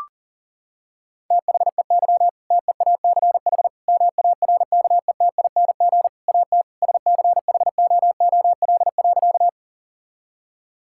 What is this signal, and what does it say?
Morse code, heard as a keyed tone, about 40 wpm, 700 Hz. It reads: THEY TEACH MARKETING AT SCHOOL.
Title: Sentences from Top 1000 Words - Review Part 5 - 40wpm